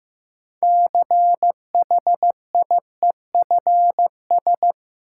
Morse code, heard as a keyed tone, about 15 wpm, 700 Hz. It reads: CHIEFS
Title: Sets of 5 Words Encore - New Word Drill - 15wpm